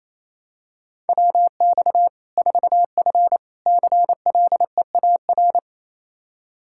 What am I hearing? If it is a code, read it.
Morse code, heard as a keyed tone, about 28 wpm, 700 Hz. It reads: WX 4F CLEAR
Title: Practice QSO - Exchange 2 - Info - 28wpm